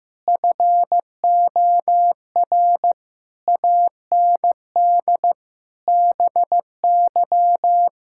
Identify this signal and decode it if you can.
Morse code, heard as a keyed tone, about 15 wpm, 700 Hz. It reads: FOR AND BY
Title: Pangram Challenge v2 - 15wpm